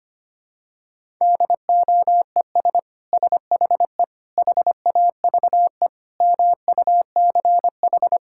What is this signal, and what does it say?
Morse code, heard as a keyed tone, about 25 wpm, 700 Hz. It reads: DOES SHE HAVE MUCH
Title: Sentences from Top 1000 Words - Review Part 6 - 25wpm